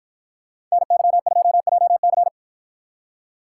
Morse code, heard as a keyed tone, about 53 wpm, 700 Hz. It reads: NY2JC